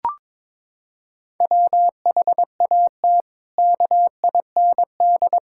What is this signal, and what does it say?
Morse code, heard as a keyed tone, about 22 wpm, 700 Hz. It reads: WHAT KIND
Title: Sentences from Top 1000 Words - Review Part 5 - 22wpm